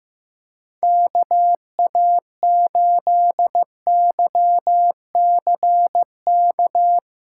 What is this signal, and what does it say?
Morse code, heard as a keyed tone, about 15 wpm, 700 Hz. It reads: KA8YCK
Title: USA Callsign DE Callsign - 6 Characters - 15wpm